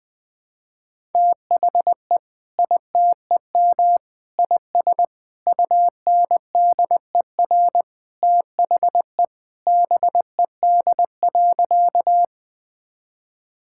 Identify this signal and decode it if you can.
Morse code, heard as a keyed tone, about 21 wpm, 700 Hz. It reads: THE ITEM IS UNDER THE BED.
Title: Sentences from Top 1000 Words - Review Part 6 - 20wpm